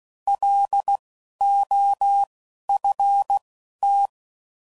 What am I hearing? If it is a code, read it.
Morse code, heard as a keyed tone, about 16 wpm, spaced as for 11 wpm, 800 Hz. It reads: LOFT
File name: Prefixes / By First Letter / Ls